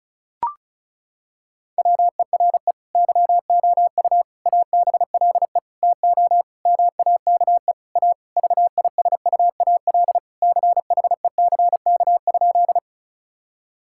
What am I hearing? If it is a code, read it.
Morse code, heard as a keyed tone, about 36 wpm, 700 Hz. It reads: WERE YOU ABLE TO MAKE A VISUAL CHECK?
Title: Sentences from Top 1000 Words - Review Part 6 - 35wpm